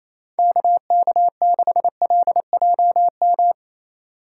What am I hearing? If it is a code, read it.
Morse code, heard as a keyed tone, about 34 wpm, 700 Hz. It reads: KK6LJM
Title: Speed Racing - USA 2x3 Call Signs 28wpm